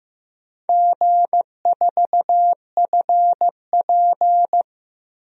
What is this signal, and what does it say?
Morse code, heard as a keyed tone, about 15 wpm, 700 Hz. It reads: G4FP
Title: DXCC-100 Call Signs - 4 Characters - Repeated 3x - 15wpm